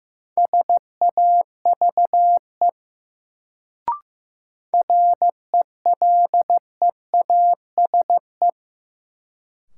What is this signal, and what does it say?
Morse code, heard as a keyed tone, about 16 wpm, 700 Hz. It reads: SAVE RELEASE
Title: Sentences from Top 1100 Words - New Word Drill Alt - 15wpm